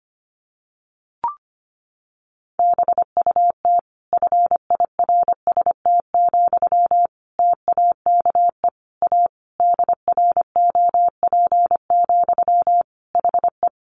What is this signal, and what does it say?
Morse code, heard as a keyed tone, about 25 wpm, 700 Hz. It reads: BUT FIRST, TAKE A DROP, HE
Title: Sentences from Top 1400 Words 25wpm